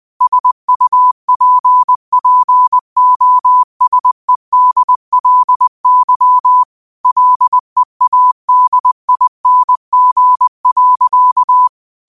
Morse code, heard as a keyed tone, about 20 wpm, 1000 Hz. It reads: SUPPOSEDLY LEADING.
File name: qotdfiles / 20wpm